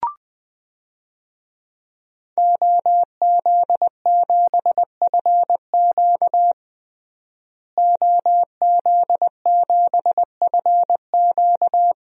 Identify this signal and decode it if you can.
Morse code, heard as a keyed tone, about 20 wpm, 700 Hz. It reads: OZ7FQ OZ7FQ
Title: DXCC-100 Call Signs - 5 Characters - Repeated 2x - 20wpm